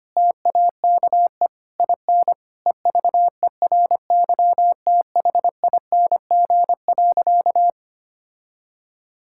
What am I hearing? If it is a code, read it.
Morse code, heard as a keyed tone, about 25 wpm, 700 Hz. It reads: TAKE IN EVERYTHING.
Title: Sentences from Top 1600 Words 25wpm